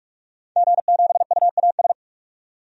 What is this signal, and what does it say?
Morse code, heard as a keyed tone, about 46 wpm, 700 Hz. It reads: G7UAS
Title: DXCC-100 Call Signs - 5 Characters - 45wpm